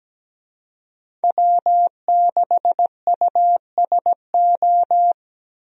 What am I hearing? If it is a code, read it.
Morse code, heard as a keyed tone, about 17 wpm, 700 Hz. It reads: W6USO
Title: USA Call Signs - 5 Characters - 17wpm